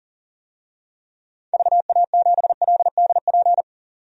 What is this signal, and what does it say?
Morse code, heard as a keyed tone, about 40 wpm, 700 Hz. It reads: VA7LDP
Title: DXCC-100 Call Signs - 6 Characters - 40wpm